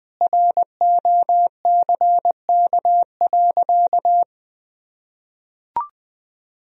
Morse code, heard as a keyed tone, about 20 wpm, 700 Hz. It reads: ROCK.
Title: Sentences from Top 1000 Words - Review Part 2 - 20wpm